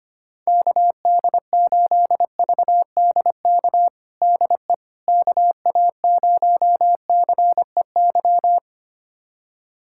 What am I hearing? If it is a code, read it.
Morse code, heard as a keyed tone, about 25 wpm, 700 Hz. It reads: KD8VDK DE KA0CEY
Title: USA Callsign DE Callsign 25wpm